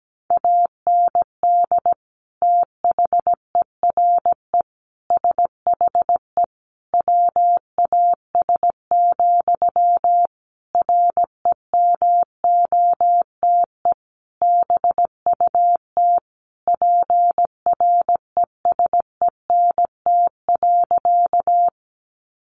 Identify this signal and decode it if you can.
Morse code, heard as a keyed tone, about 17 wpm, 700 Hz. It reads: AND THERE SHE WAS, REMOTE BUT PRESENT.
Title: Sentences from Top 1300 Words 17wpm